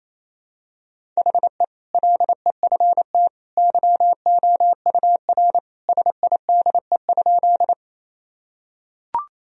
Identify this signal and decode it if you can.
Morse code, heard as a keyed tone, about 28 wpm, 700 Hz. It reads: HE LEFT YOUR SIDE?
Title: Sentences from Top 1000 Words - Review Part 1 - 28wpm